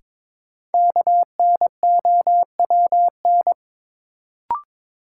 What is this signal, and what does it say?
Morse code, heard as a keyed tone, about 22 wpm, 700 Hz. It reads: KNOWN
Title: Sentences from Top 600 Words - New Word Drill - 22wpm